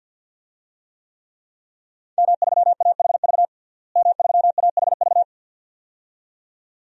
Morse code, heard as a keyed tone, about 51 wpm, 700 Hz. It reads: M3AHV M3AHV
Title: DXCC-100 Call Signs - 5 Characters - Repeated 2x - 50wpm